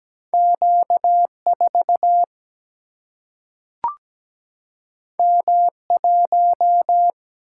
Morse code, heard as a keyed tone, about 17 wpm, 700 Hz. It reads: Q 4 M 1
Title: Single-Letter Single-Number 17wpm